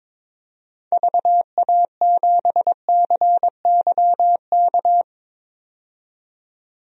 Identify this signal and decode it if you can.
Morse code, heard as a keyed tone, about 22 wpm, 700 Hz. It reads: VA7CYK
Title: DXCC-100 Call Signs - 6 Characters - 22wpm